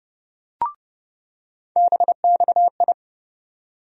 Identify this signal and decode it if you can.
Morse code, heard as a keyed tone, about 29 wpm, 700 Hz. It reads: BXI
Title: USA Call Sign Postfixes - 3 Letters 30wpm